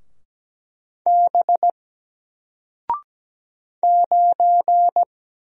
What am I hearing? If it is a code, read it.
Morse code, heard as a keyed tone, about 17 wpm, 700 Hz. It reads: B 9
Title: Single Letter-Number 17wpm